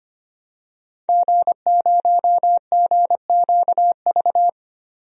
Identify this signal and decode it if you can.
Morse code, heard as a keyed tone, about 25 wpm, 700 Hz. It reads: G0GQV
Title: DXCC-100 Call Signs - 5 Characters - 25wpm